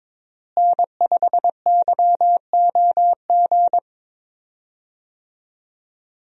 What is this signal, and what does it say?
Morse code, heard as a keyed tone, about 22 wpm, 700 Hz. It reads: N5YOG
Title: USA Call Signs - 5 Characters - Repeated 2x - 22wpm